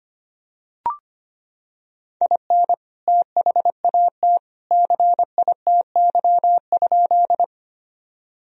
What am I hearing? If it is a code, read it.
Morse code, heard as a keyed tone, about 25 wpm, 700 Hz. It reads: IN THAT CITY?
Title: Sentences from Top 1000 Words - Review Part 1 - 25wpm